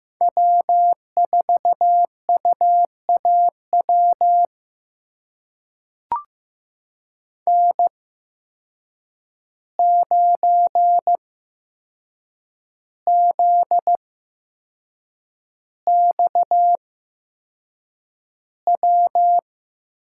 Morse code, heard as a keyed tone, about 15 wpm, spaced as for 6 wpm, 700 Hz. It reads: W4UAW N9ZXW
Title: USA Call Signs - 5 Characters - 8x Character Spacing - 15wpm